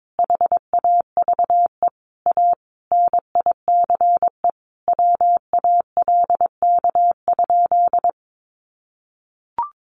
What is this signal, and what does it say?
Morse code, heard as a keyed tone, about 22 wpm, 700 Hz. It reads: HAVE A NICE WALK?
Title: Sentences from Top 1100 Words 22wpm